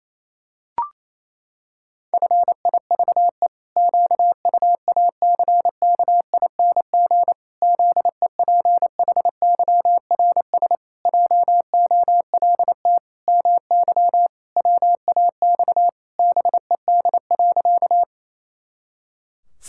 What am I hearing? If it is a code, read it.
Morse code, heard as a keyed tone, about 27 wpm, 700 Hz. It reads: FIVE QUACKING ZEPHYRS JOLT MY WAX BED.
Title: Pangram Challenge v2 - 28wpm